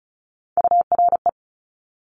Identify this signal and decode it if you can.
Morse code, heard as a keyed tone, about 36 wpm, 700 Hz. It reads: URE
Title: Top Three Letter Combinations in Words 35wpm